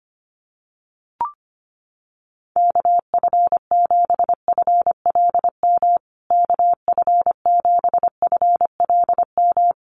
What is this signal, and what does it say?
Morse code, heard as a keyed tone, about 25 wpm, 700 Hz. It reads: KF7FLM KF7FLM
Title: USA Call Signs Repeated 3x 25wpm